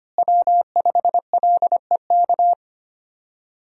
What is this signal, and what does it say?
Morse code, heard as a keyed tone, about 25 wpm, 700 Hz. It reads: W5LEK
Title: USA Call Signs - 5 Characters - Repeated 3x - 25wpm